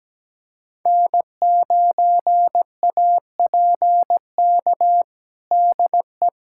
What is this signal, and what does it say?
Morse code, heard as a keyed tone, about 17 wpm, 700 Hz. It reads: N9APK DE
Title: USA Callsign DE Callsign - 5 Characters - 17wpm